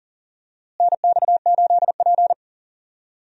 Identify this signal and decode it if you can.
Morse code, heard as a keyed tone, about 40 wpm, 700 Hz. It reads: NX8P
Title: USA Call Signs - 4 Characters - Repeated 2x - 40wpm